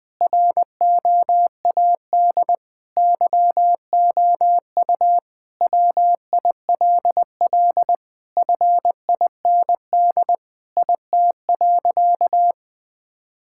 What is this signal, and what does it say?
Morse code, heard as a keyed tone, about 20 wpm, 700 Hz. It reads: ROAD YOU WILL FIND IT.
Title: Sentences with 3-Word Phrases - Spoken Beginning - 20wpm